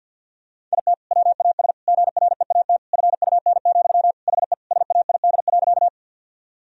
Morse code, heard as a keyed tone, about 50 wpm, 700 Hz. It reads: IT WAS GREAT FUN, HE SAID.
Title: Sentences from Top 1000 Words - Review Part 4 - 50wpm